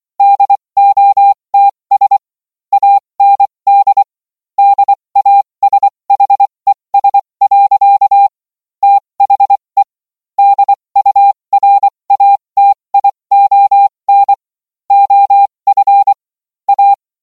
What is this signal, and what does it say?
Morse code, heard as a keyed tone, about 24 wpm, spaced as for 20 wpm, 800 Hz. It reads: DOTS AND DASHES. THE DURATION OF A